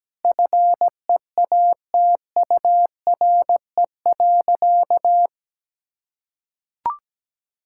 Morse code, heard as a keyed tone, about 17 wpm, 700 Hz. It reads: FEATURE.
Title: Sentences from Top 1000 Words - Review Part 5 - 17wpm